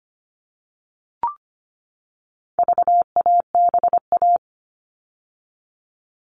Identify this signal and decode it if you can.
Morse code, heard as a keyed tone, about 25 wpm, 700 Hz. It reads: VABA
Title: All Cut Numbers - 4-Digit 25wpm